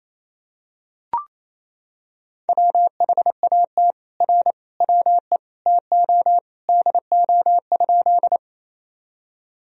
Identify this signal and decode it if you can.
Morse code, heard as a keyed tone, about 28 wpm, 700 Hz. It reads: WHAT R WE TO DO?
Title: Sentences from Top 100 Words and Abbreviations 28wpm